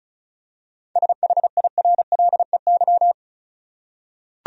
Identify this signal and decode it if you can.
Morse code, heard as a keyed tone, about 36 wpm, 700 Hz. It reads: SHIRLEY
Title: USA Common Names - 35wpm